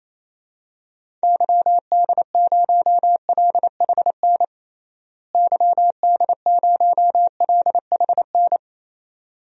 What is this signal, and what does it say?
Morse code, heard as a keyed tone, about 28 wpm, 700 Hz. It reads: YD0LHN YD0LHN
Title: DXCC-100 Call Signs - 6 Characters - Repeated 2x - 28wpm